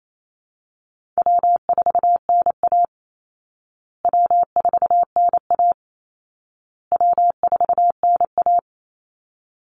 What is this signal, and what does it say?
Morse code, heard as a keyed tone, about 28 wpm, 700 Hz. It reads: W4NA W4NA W4NA
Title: Call Signs Repeated 2x - 28wpm